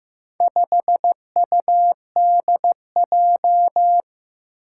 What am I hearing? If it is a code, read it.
Morse code, heard as a keyed tone, about 16 wpm, 700 Hz. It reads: 5UDJ
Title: Four Challenging Letter-Number Combinations 15wpm